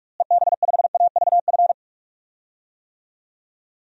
Morse code, heard as a keyed tone, about 46 wpm, 700 Hz. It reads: EB5AVF
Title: DXCC-100 Call Signs - 6 Characters - Repeated 2x - 45wpm